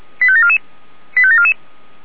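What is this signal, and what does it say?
Five-tone selective call (ZVEI), heard as quick runs of five tones, tones of about 80 ms. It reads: 85744 85744
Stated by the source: ZVEI five-tone selective call 85753, received by SDR.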